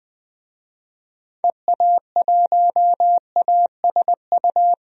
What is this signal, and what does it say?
Morse code, heard as a keyed tone, about 20 wpm, 700 Hz. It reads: EA1ASU
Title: DXCC-100 Call Signs - 6 Characters - 20wpm